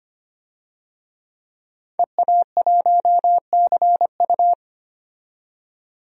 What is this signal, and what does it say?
Morse code, heard as a keyed tone, about 25 wpm, 700 Hz. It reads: EA1CU
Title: DXCC-100 Call Signs - 5 Characters - 25wpm